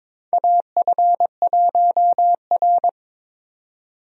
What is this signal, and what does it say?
Morse code, heard as a keyed tone, about 22 wpm, 700 Hz. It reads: AF1R
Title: USA Call Signs - 4 Characters - 22wpm